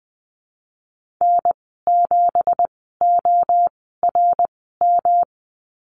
Morse code, heard as a keyed tone, about 20 wpm, spaced as for 15 wpm, 700 Hz. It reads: N7ORM
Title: 186 Lesson 30g - Letter b - Call Signs 20wpm